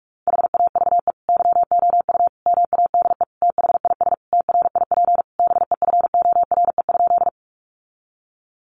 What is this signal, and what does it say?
Morse code, heard as a keyed tone, about 46 wpm, 700 Hz. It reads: HAVE YOU MADE THIS TRIP BEFORE?
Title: Sentences from Top 1300 Words 45wpm